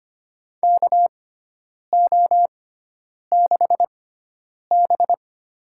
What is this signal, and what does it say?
Morse code, heard as a keyed tone, about 25 wpm, spaced as for 9 wpm, 700 Hz. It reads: KO6B
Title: USA Call Signs - 4 Characters - 6x Character Spacing - 25wpm